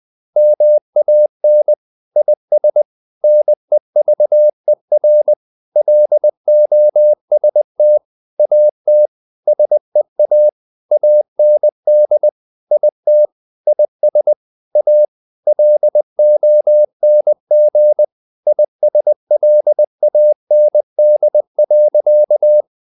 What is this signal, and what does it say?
Morse code, heard as a keyed tone, about 20 wpm, 600 Hz. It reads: MAN IS NEVER LOST AT SEA AND IT IS A LONG ISLAND.
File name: the old man and the sea part 2 20wpm